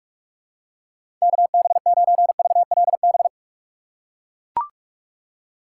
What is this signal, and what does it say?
Morse code, heard as a keyed tone, about 42 wpm, 700 Hz. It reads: KB9VLB
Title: USA Call Signs - 6 Characters - 4x Character Spacing - 45wpm